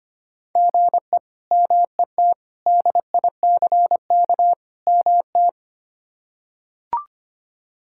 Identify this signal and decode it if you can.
Morse code, heard as a keyed tone, about 25 wpm, 700 Hz. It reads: GE MET DICK MT
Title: K1USN SST QSO Practice - 25wpm